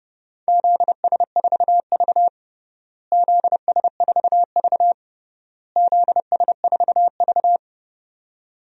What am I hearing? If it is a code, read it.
Morse code, heard as a keyed tone, about 32 wpm, 700 Hz. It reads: ZS4V ZS4V ZS4V
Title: DXCC-100 Call Signs - 4 Characters - Repeated 3x - 30wpm